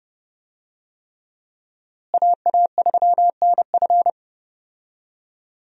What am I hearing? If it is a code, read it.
Morse code, heard as a keyed tone, about 30 wpm, 700 Hz. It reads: AA3NF
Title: USA Call Signs - 5 Characters - Repeated 3x - 30wpm